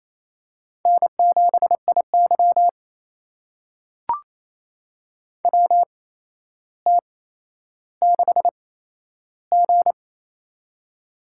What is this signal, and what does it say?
Morse code, heard as a keyed tone, about 28 wpm, spaced as for 12 wpm, 700 Hz. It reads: N7IY WT6G
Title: USA Call Signs - 8x Character Spacing - 28wpm